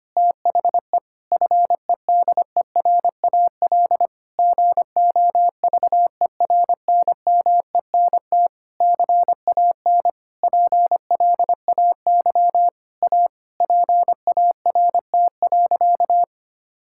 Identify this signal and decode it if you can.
Morse code, heard as a keyed tone, about 25 wpm, 700 Hz. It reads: THE FEDERAL GOVERNMENT CAN PLAY A PART.
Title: Sentences from Top 1000 Words - Review Part 5 - 25wpm